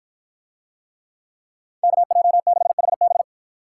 Morse code, heard as a keyed tone, about 53 wpm, 700 Hz. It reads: KJ6HB